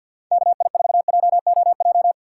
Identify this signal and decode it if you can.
Morse code, heard as a keyed tone, about 46 wpm, 700 Hz. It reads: KI4JOJ
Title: USA Call Signs - 6 Characters - 3x Character Spacing - 50wpm